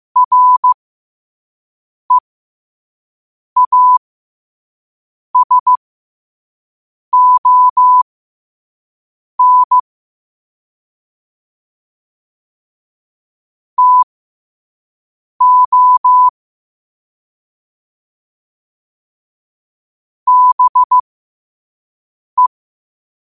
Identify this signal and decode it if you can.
Morse code, heard as a keyed tone, about 15 wpm, spaced as for 5 wpm, 1000 Hz. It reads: REASON TO BE
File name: qotdfiles / 5wpm